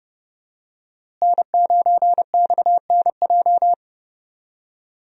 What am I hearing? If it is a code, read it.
Morse code, heard as a keyed tone, about 30 wpm, 700 Hz. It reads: N9XNJ
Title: USA Call Signs - 5 Characters - Repeated 3x - 30wpm